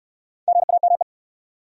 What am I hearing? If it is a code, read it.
Morse code, heard as a keyed tone, about 66 wpm, 700 Hz. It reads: GAME